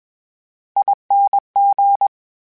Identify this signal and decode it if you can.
Morse code, heard as a keyed tone, about 20 wpm, 800 Hz. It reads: ING
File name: Special INGs-mp3s / Special INGs-mp3s / VerySlowToFaster Ing @20